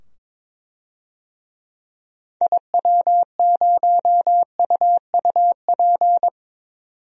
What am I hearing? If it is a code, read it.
Morse code, heard as a keyed tone, about 22 wpm, 700 Hz. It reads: IW0UUP
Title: DXCC-100 Call Signs - 6 Characters - 22wpm